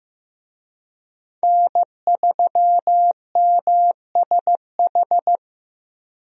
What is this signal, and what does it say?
Morse code, heard as a keyed tone, about 15 wpm, 700 Hz. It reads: N3MSH
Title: USA Call Signs - 5 Characters - 15wpm